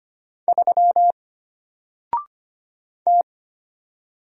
Morse code, heard as a keyed tone, about 25 wpm, 700 Hz. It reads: 3 T
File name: Single-Letter-Number-25wpm